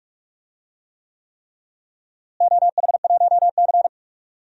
Morse code, heard as a keyed tone, about 44 wpm, 700 Hz. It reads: OH1C
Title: DXCC-100 Call Signs - 4 Characters - Repeated 3x - 45wpm